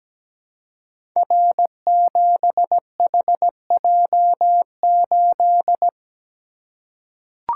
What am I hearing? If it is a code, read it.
Morse code, heard as a keyed tone, about 17 wpm, 700 Hz. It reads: R7HJ8
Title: Code Groups - 5 Letter:Numbers - 17wpm